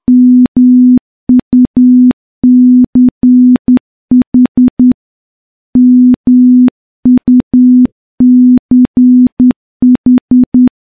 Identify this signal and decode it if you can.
Morse code, heard as a keyed tone, about 10 wpm, 250 Hz. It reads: MUCH MUCH
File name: morse-training-10WPM